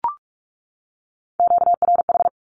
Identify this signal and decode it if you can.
Morse code, heard as a keyed tone, about 44 wpm, 700 Hz. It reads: QRH
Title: Q-Codes 45wpm